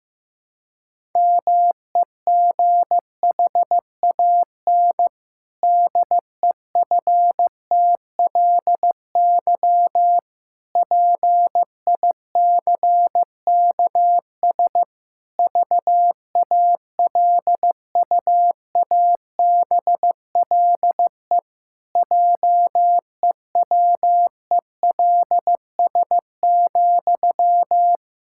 Morse code, heard as a keyed tone, about 15 wpm, 700 Hz. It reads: MEGHAN DEFTLY PICKS VALUABLE JEWELS,
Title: Pangram Challenge v2 - 15wpm